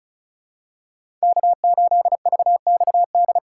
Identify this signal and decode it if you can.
Morse code, heard as a keyed tone, about 36 wpm, 700 Hz. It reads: K8VXD
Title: USA Call Signs - 5 Characters - 3x Character Spacing - 35wpm